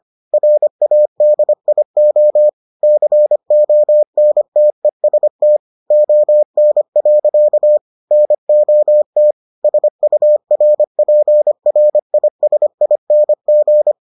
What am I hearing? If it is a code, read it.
Morse code, heard as a keyed tone, about 25 wpm, 600 Hz. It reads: RADIO CONTEST ON. NOT SURPRISING